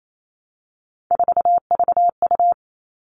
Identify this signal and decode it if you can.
Morse code, heard as a keyed tone, about 28 wpm, 700 Hz. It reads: 4VU
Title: All Cut Numbers - 3-Digit 28wpm